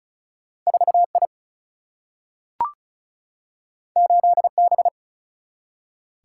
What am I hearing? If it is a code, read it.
Morse code, heard as a keyed tone, about 36 wpm, 700 Hz. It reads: 4 I 8 B
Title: Single-Letter Single-Number 35wpm